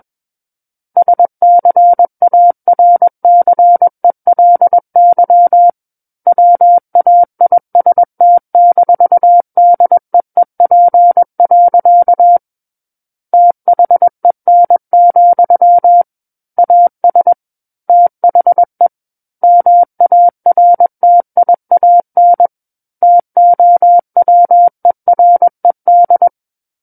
Morse code, heard as a keyed tone, about 21 wpm, 700 Hz. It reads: SCARCELY WAIST-DEEP. THEN, AS THE MARTIAN TOWERED
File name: Book One / 12-Chapter 12 - 21 WPM